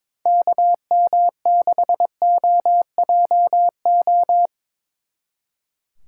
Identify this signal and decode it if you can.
Morse code, heard as a keyed tone, about 26 wpm, 700 Hz. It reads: KM6OJO
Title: Speed Racing - USA 2x3 Call Signs 22wpm